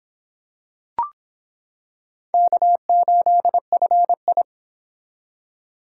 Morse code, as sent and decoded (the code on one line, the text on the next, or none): -.- ---.. ..-. ..
K8FI